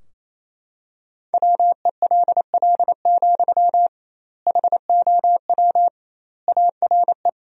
Morse code, heard as a keyed tone, about 28 wpm, 700 Hz. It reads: WELL, HOW ARE
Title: 2x Word-Spacing - Sentences from Top 200 Words - 28wpm